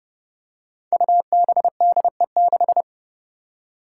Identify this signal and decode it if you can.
Morse code, heard as a keyed tone, about 32 wpm, 700 Hz. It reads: UBDE6
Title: Five Challenging Letter-Number Combinations 30wpm